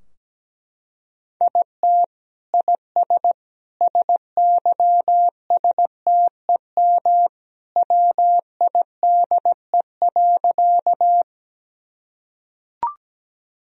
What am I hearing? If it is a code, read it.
Morse code, heard as a keyed tone, about 17 wpm, 700 Hz. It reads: IT IS SYSTEM WIDE.
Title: Sentences from Top 1000 Words - Review Part 3 - 17wpm